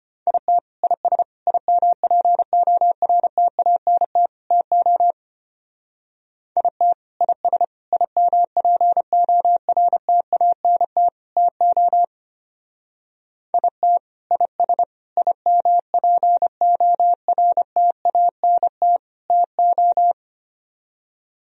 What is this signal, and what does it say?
Morse code, heard as a keyed tone, about 32 wpm, 700 Hz. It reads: IT IS IMPORTANT TO IT IS IMPORTANT TO IT IS IMPORTANT TO